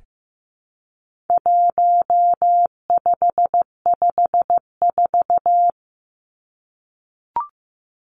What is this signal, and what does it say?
Morse code, heard as a keyed tone, about 15 wpm, 700 Hz. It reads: 1554
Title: Numbers - 4-Digit 15wpm